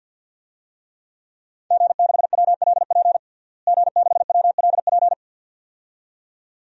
Morse code, heard as a keyed tone, about 51 wpm, 700 Hz. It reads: G6WLP G6WLP
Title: DXCC-100 Call Signs - 5 Characters - Repeated 2x - 50wpm